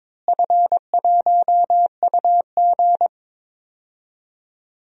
Morse code, heard as a keyed tone, about 22 wpm, 700 Hz. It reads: F1UG
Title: DXCC-100 Call Signs - 4 Characters - Repeated 3x - 22wpm